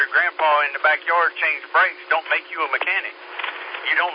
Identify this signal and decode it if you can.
Emergency Alert System (SAME) tones, heard as harsh buzzing bursts, with speech. It is heard from 0.7 to 3.2 s.